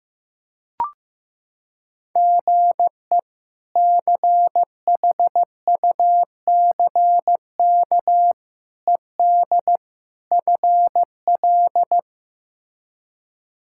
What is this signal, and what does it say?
Morse code, heard as a keyed tone, about 15 wpm, 700 Hz. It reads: GE CHUCK ED FL
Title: K1USN SST QSO Practice - 15wpm